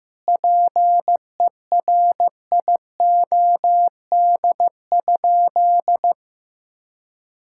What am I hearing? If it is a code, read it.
Morse code, heard as a keyed tone, about 16 wpm, 700 Hz. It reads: PERIOD?
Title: Sentences from Top 1000 Words - Review Part 5 - 15wpm